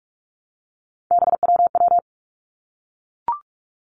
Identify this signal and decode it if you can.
Morse code, heard as a keyed tone, about 46 wpm, 700 Hz. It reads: BWW
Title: Three Challenging Letter-Number Combinations 45wpm